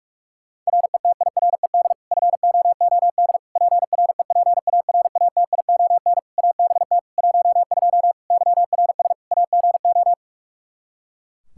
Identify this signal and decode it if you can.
Morse code, heard as a keyed tone, about 44 wpm, 700 Hz. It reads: RETIRED FOOD PREPARATION ABT 12 YRS AGO
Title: K1USN SST QSO Practice 45wpm